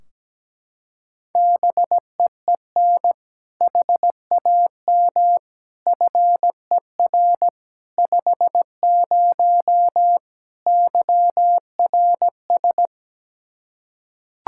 Morse code, heard as a keyed tone, about 17 wpm, 700 Hz. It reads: BEEN HAM FER 50 YRS
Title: K1USN SST QSO Practice 17wpm